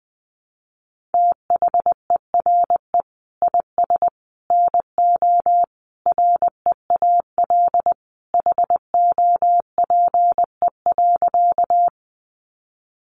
Morse code, heard as a keyed tone, about 20 wpm, 700 Hz. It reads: THERE IS NO REAL HOPE.
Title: Sentences from Top 1000 Words 20wpm